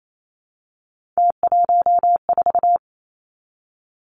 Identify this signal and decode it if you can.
Morse code, heard as a keyed tone, about 28 wpm, 700 Hz. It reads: T14
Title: Three Challenging Letter-Number Combinations 28wpm